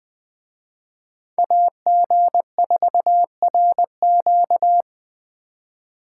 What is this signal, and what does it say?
Morse code, heard as a keyed tone, about 21 wpm, 700 Hz. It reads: AG4RQ